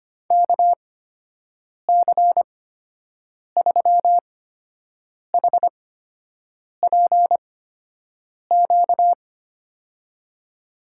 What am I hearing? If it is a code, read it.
Morse code, heard as a keyed tone, about 25 wpm, spaced as for 7 wpm, 700 Hz. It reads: KC3HPQ
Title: USA Call Signs - 6 Characters - 8x Character Spacing - 25wpm